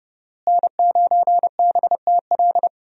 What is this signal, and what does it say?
Morse code, heard as a keyed tone, about 30 wpm, 700 Hz. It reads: N9BTL
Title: USA Call Signs - 5 Characters - Repeated 3x - 30wpm